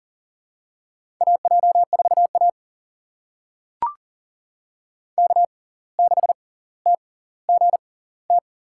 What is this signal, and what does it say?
Morse code, heard as a keyed tone, about 40 wpm, spaced as for 19 wpm, 700 Hz. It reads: AJ4A K6TGT
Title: USA Call Signs - 6x Character Spacing - 40wpm